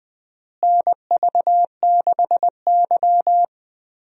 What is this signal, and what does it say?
Morse code, heard as a keyed tone, about 20 wpm, 700 Hz. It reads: NV6Y
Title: USA Call Signs - 4 Characters - Repeated 3x - 20wpm